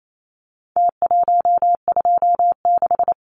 Morse code, heard as a keyed tone, about 28 wpm, 700 Hz. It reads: T126
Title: Numbers - Contest Style - 4-Digit 28wpm